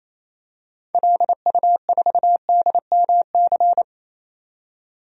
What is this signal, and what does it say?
Morse code, heard as a keyed tone, about 28 wpm, 700 Hz. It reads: LU4DMC
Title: DXCC-100 Call Signs - 6 Characters - 28wpm